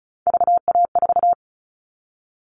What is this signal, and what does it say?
Morse code, heard as a keyed tone, about 34 wpm, 700 Hz. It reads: VA4
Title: All Cut Numbers - 3-Digit 35wpm